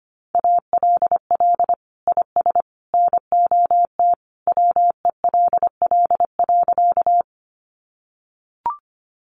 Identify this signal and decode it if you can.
Morse code, heard as a keyed tone, about 25 wpm, 700 Hz. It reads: ALL IS NOT WELL.
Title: Sentences from Top 600 Words 25wpm